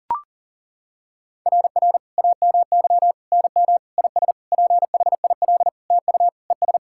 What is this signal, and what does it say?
Morse code, heard as a keyed tone, about 40 wpm, 700 Hz. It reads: RR AMY NM IS PHIL TU ES
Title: SOTA QSO Practice 40wpm